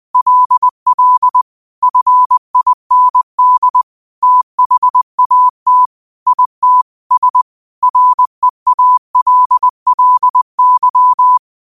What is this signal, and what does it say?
Morse code, heard as a keyed tone, about 20 wpm, 1000 Hz. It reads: LL FIND THAT IT S REALLY
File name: qotdfiles / 20wpm